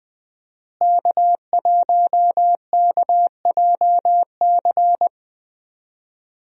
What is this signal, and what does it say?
Morse code, heard as a keyed tone, about 21 wpm, 700 Hz. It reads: K1KJC
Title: USA Call Signs - 5 Characters - Repeated 3x - 20wpm